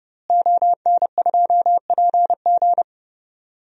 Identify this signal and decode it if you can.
Morse code, heard as a keyed tone, about 32 wpm, 700 Hz. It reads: ON2PG
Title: DXCC-100 Call Signs - 5 Characters - Repeated 3x - 30wpm